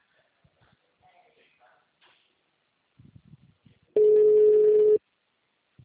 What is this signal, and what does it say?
Ringback tone (Europe, 425 Hz), with speech.
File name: out- -2026- - - .76266